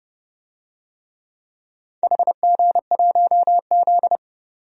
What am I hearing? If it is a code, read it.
Morse code, heard as a keyed tone, about 32 wpm, 700 Hz. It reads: HG1Z
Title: DXCC-100 Call Signs - 4 Characters - 30wpm